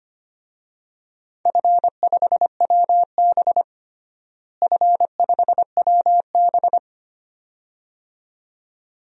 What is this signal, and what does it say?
Morse code, heard as a keyed tone, about 25 wpm, 700 Hz. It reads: F5WB F5WB
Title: DXCC-100 Call Signs - 4 Characters - Repeated 2x - 25wpm